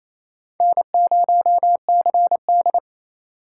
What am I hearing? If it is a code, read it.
Morse code, heard as a keyed tone, about 28 wpm, 700 Hz. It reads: N0CD
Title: USA Call Signs - 4 Characters - 2x Character Spacing - 28wpm